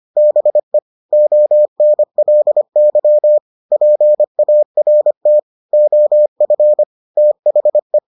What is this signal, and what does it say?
Morse code, heard as a keyed tone, about 25 wpm, 600 Hz. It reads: BE ONLY PART OF THE